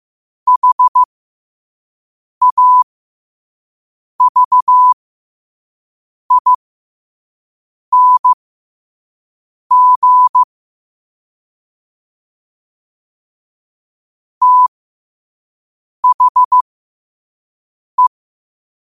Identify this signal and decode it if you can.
Morse code, heard as a keyed tone, about 15 wpm, spaced as for 5 wpm, 1000 Hz. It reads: HAVING THE
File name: qotdfiles / 5wpm